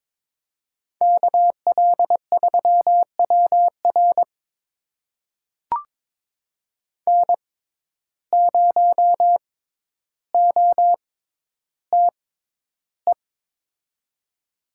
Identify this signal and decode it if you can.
Morse code, heard as a keyed tone, about 22 wpm, spaced as for 11 wpm, 700 Hz. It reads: KL3WR N0OTE
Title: USA Call Signs - 5 Characters - 6x Character Spacing - 22wpm